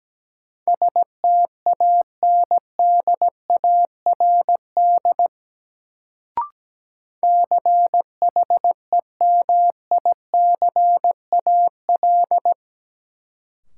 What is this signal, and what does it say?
Morse code, heard as a keyed tone, about 17 wpm, 700 Hz. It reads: STANDARD CHEMICAL
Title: Sentences from Top 1200 Words - New Word Drill Alt - 17wpm